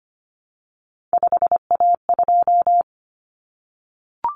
Morse code, heard as a keyed tone, about 25 wpm, 700 Hz. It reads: 5A2
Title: Common Cut Numbers - 3-Digit 25wpm